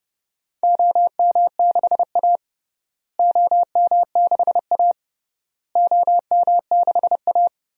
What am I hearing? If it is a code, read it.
Morse code, heard as a keyed tone, about 32 wpm, 700 Hz. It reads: OM6A OM6A OM6A
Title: DXCC-100 Call Signs - 4 Characters - Repeated 3x - 30wpm